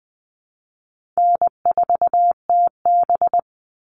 Morse code, heard as a keyed tone, about 20 wpm, 700 Hz. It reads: N4TB
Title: Easy Call Signs - 20wpm